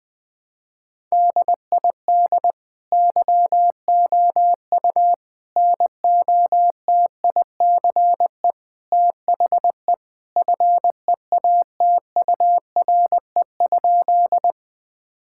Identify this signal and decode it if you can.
Morse code, heard as a keyed tone, about 20 wpm, 700 Hz. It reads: DID YOU NOTICE THE FEATURE?
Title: Sentences from Top 1000 Words - Review Part 5 - 20wpm